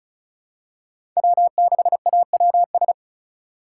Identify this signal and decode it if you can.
Morse code, heard as a keyed tone, about 36 wpm, 700 Hz. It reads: W6AWS
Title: K1USN SST QSO Practice - 35wpm